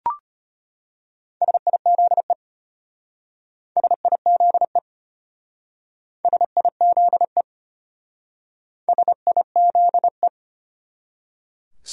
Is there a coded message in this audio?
Yes: Morse code from 1.4 to 10.3 s, heard as a keyed tone, about 32 wpm, 700 Hz. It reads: SIZE SIZE SIZE SIZE